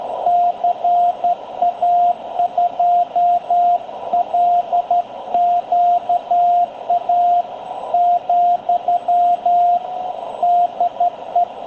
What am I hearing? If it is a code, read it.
Morse code, heard as a keyed tone, about 12 wpm, 700 Hz, with radio noise. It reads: CA2LQA , DE
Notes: Encuentro de radioaficionados en 7030, banda de 40 m, batiendo manipuladores, para despedir a Tony, CA2LQA, de Villa Alemana, Valparaíso, Chile.